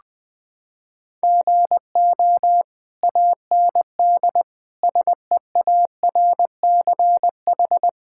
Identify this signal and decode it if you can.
Morse code, heard as a keyed tone, about 21 wpm, 700 Hz. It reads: GO AND SEARCH
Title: Sentences from Top 1000 Words - Review Part 5 - 20wpm